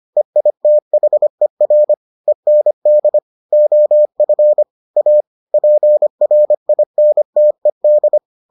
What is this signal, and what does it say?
Morse code, heard as a keyed tone, about 25 wpm, 600 Hz. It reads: EITHER END OF A PRINTED